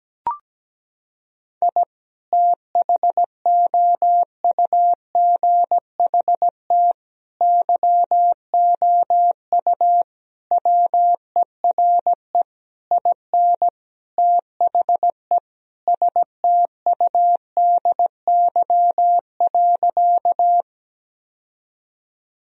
Morse code, heard as a keyed tone, about 17 wpm, 700 Hz. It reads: I THOUGHT YOU WERE IN THE STUDY.
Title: Sentences from Top 1000 Words - Review Part 1 - 17wpm